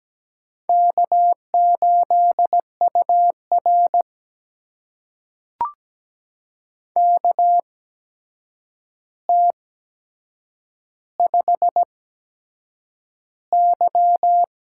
Morse code, heard as a keyed tone, about 17 wpm, spaced as for 7 wpm, 700 Hz. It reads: K8UR KT5Y
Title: USA Call Signs - 4 Characters - 8x Character Spacing - 17wpm